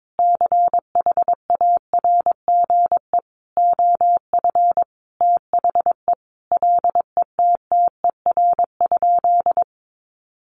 Morse code, heard as a keyed tone, about 22 wpm, 700 Hz. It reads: CHARGE OF THE LETTER?
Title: Sentences from Top 900 Words 22wpm